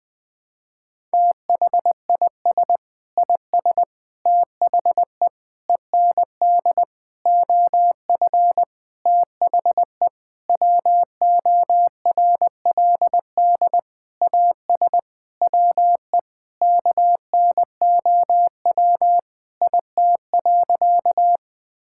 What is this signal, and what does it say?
Morse code, heard as a keyed tone, about 20 wpm, 700 Hz. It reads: THIS IS THE END OF THE WORLD AS WE KNOW IT.
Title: Sentences with 9-Word Phrases - 20wpm